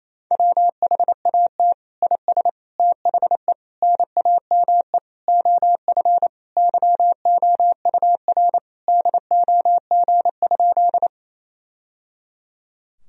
Morse code, heard as a keyed tone, about 28 wpm, 700 Hz. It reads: WHAT IS THE NAME OF YOUR DOG?
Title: Sentences from Top 1000 Words - Review Part 2 - 28wpm